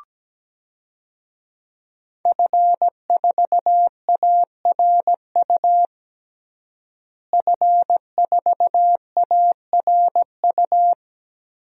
Morse code, heard as a keyed tone, about 17 wpm, 700 Hz. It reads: F4ARU F4ARU
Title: DXCC-100 Call Signs - 5 Characters - Repeated 3x - 17wpm